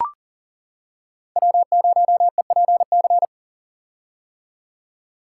Morse code, heard as a keyed tone, about 37 wpm, 700 Hz. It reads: W0EPC
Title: K1USN SST QSO Practice - 40wpm